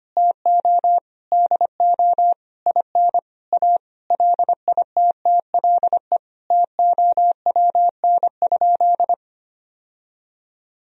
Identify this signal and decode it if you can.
Morse code, heard as a keyed tone, about 25 wpm, 700 Hz. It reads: TO DO IN A LITTLE TOWN?
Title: Sentences from Top 1000 Words - Review Part 2 - 25wpm